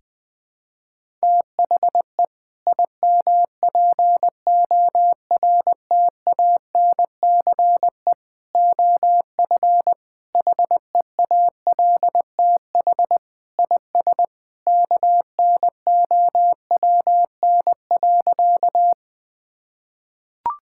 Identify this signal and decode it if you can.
Morse code, heard as a keyed tone, about 20 wpm, 700 Hz. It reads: THE IMPORTANCE OF HEALTH IS KNOWN.
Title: Sentences with 3-Word Phrases - 20wpm